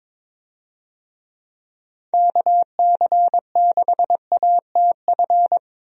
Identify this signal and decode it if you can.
Morse code, heard as a keyed tone, about 22 wpm, 700 Hz. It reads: KC6ATF
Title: USA Call Signs - 6 Characters - Repeated 3x - 22wpm